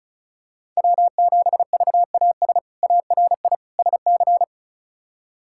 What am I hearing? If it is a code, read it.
Morse code, heard as a keyed tone, about 36 wpm, 700 Hz. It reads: W7VAS ARI SC
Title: K1USN SST QSO Practice - 35wpm